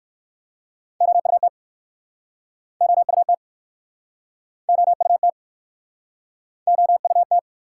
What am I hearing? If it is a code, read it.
Morse code, heard as a keyed tone, about 59 wpm, 700 Hz. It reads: OUT OUT OUT OUT